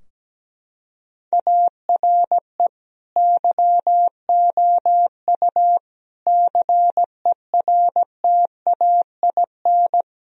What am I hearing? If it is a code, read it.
Morse code, heard as a keyed tone, about 17 wpm, 700 Hz. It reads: ARE YOU CERTAIN
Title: Sentences from Top 1000 Words - Review Part 2 - 17wpm